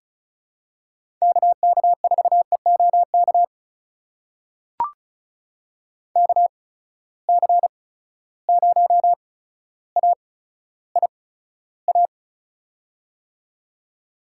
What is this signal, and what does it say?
Morse code, heard as a keyed tone, about 36 wpm, spaced as for 15 wpm, 700 Hz. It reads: KK4EOK KC0AIA
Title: USA Call Signs - 6 Characters - 8x Character Spacing - 35wpm